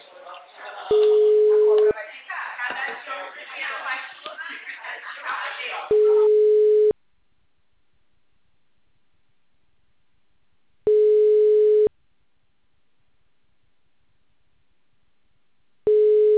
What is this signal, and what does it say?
Ringback tone (Europe, 425 Hz), with speech.
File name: out- -2010- - - .222